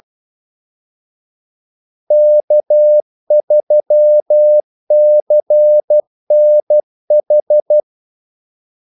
Morse code, heard as a keyed tone, about 12 wpm, 600 Hz. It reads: K3CNH
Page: 2 ekstra ordmellemrum mellem kaldesignalerne